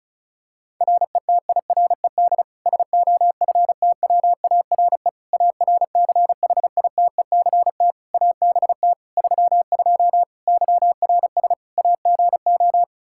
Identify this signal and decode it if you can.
Morse code, heard as a keyed tone, about 36 wpm, 700 Hz. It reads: RETIRED SOFTWARE ARCHITECT ABT 32 YRS AGO
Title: Practice QSO - Exchange 3 - More Info - 35wpm